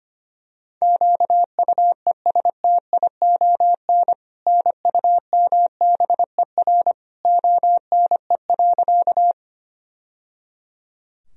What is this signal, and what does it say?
Morse code, heard as a keyed tone, about 25 wpm, 700 Hz. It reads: QUESTION NUMBER ONE.
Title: Sentences from Top 1000 Words - Review Part 1 - 25wpm